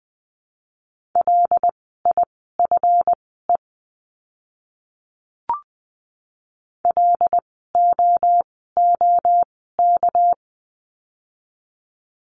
Morse code, heard as a keyed tone, about 20 wpm, spaced as for 14 wpm, 700 Hz. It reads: LIFE LOOK
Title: 230 Lesson 37f - Letter x - Words 20wpm